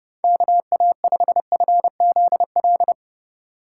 Morse code, heard as a keyed tone, about 29 wpm, 700 Hz. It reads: KA5FZL